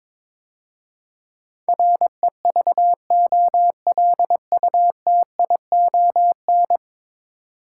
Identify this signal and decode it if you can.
Morse code, heard as a keyed tone, about 22 wpm, 700 Hz. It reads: REVOLUTION
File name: practice / 22wpm / 10 Letter Words - First 3 Letters Given - 22wpm